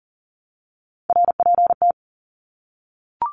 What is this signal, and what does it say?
Morse code, heard as a keyed tone, about 40 wpm, 700 Hz. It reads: RPT
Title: Abbreviations 40wpm